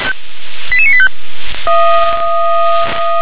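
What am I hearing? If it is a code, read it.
Five-tone selective call (ZVEI), heard as quick runs of five tones, tones of about 70 ms. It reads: 88985
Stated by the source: ZVEI five-tone call 88634 received by SDR.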